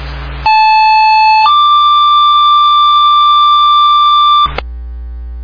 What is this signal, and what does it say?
Two-tone page, heard as a one-second tone, then a three-second tone, about 870 then 1180 Hz, with noise. It is heard from 0.5 to 4.5 s.